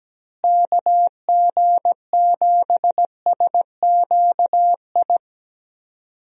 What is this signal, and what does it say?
Morse code, heard as a keyed tone, about 17 wpm, 700 Hz. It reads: KG7SQI
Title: USA Call Signs - 6 Characters - Repeated 3x - 17wpm